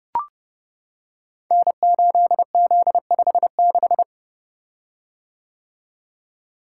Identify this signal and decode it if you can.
Morse code, heard as a keyed tone, about 30 wpm, 700 Hz. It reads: N8Z56
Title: Code Groups - 5 Letter:Numbers - 30wpm